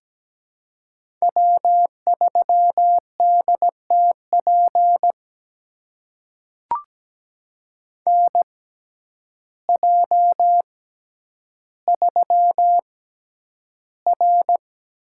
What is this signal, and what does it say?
Morse code, heard as a keyed tone, about 17 wpm, spaced as for 9 wpm, 700 Hz. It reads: W3DTP NJ3R
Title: USA Call Signs - 6x Character Spacing - 17wpm